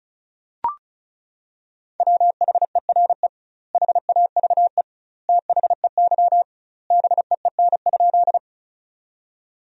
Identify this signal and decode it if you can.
Morse code, heard as a keyed tone, about 36 wpm, 700 Hz. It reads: WHERE HAVE THEY BEEN?
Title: 2x Word-Spacing - Sentences from Top 200 Words - 35wpm